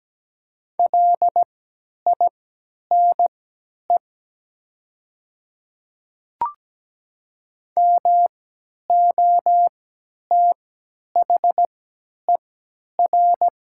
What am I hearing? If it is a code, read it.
Morse code, heard as a keyed tone, about 17 wpm, spaced as for 10 wpm, 700 Hz. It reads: LINE MOTHER
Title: 3x Character Spacing - Top 200 Words - 17wpm